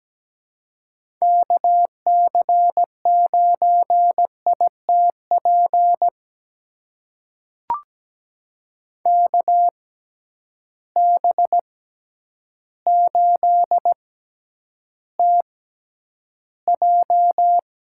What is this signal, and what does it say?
Morse code, heard as a keyed tone, about 17 wpm, spaced as for 9 wpm, 700 Hz. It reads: KC9ITP KB8TJ
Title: USA Call Signs - 6 Characters - 6x Character Spacing - 17wpm